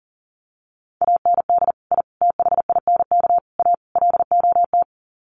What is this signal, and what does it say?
Morse code, heard as a keyed tone, about 37 wpm, 700 Hz. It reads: AND I THINK A LOT
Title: Sets of 5 Words Encore2 40wpm